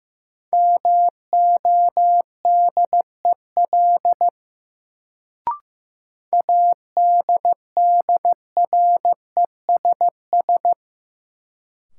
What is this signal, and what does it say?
Morse code, heard as a keyed tone, about 15 wpm, 700 Hz. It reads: MODEL ADDRESS
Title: Sentences from Top 1000 Words - New Word Drill - 15wpm